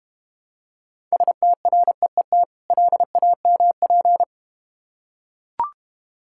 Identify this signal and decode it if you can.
Morse code, heard as a keyed tone, about 29 wpm, 690 Hz. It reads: STREET LAMP
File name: Easy Words - Speed-Racing with Hints - 25wpm